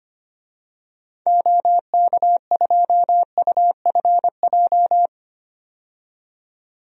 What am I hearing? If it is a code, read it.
Morse code, heard as a keyed tone, about 25 wpm, 690 Hz. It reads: OK2UFJ
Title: DXCC-100 Call Signs - 6 Characters - 25wpm